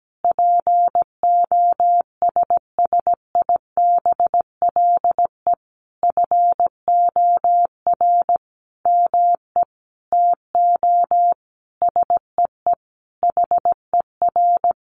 Morse code, heard as a keyed tone, about 17 wpm, 700 Hz. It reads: POSSIBLE FOR ME TO SEE HER
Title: Sentences from Top 500 Words 17wpm